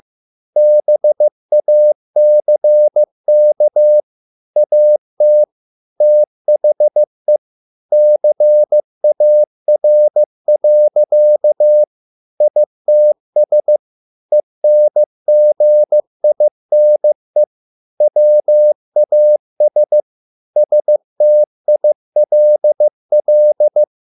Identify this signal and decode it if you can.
Morse code, heard as a keyed tone, about 15 wpm, 600 Hz. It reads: BACK AT THE CAR. ITS ENGINE WAS STILL